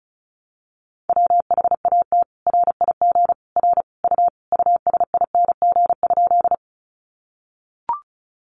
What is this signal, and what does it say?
Morse code, heard as a keyed tone, about 34 wpm, 700 Hz. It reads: WHAT RIG R U USING?
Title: Common QSO Phrases 35wpm